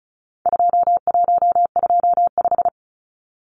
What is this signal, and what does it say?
Morse code, heard as a keyed tone, about 34 wpm, 700 Hz. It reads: 2125
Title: Numbers - 4-Digit 35wpm